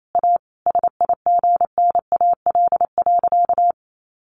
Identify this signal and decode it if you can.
Morse code, heard as a keyed tone, about 28 wpm, 700 Hz. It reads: A SIGNAL.
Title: Sentences from Top 1400 Words 28wpm